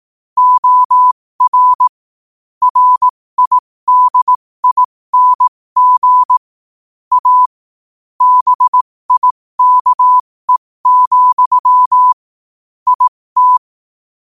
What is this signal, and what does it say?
Morse code, heard as a keyed tone, about 18 wpm, spaced as for 15 wpm, 1000 Hz. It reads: OR RIDING A BIKE, IT
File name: qotdfiles / 15wpm